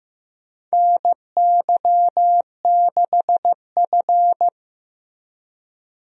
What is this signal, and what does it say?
Morse code, heard as a keyed tone, about 15 wpm, 700 Hz. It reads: NY6F
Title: USA Call Signs - 4 Characters - Repeated 3x - 15wpm